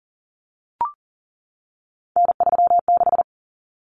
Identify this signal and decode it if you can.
Morse code, heard as a keyed tone, about 40 wpm, 700 Hz. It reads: N36
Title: Common Cut Numbers - 3-Digit 40wpm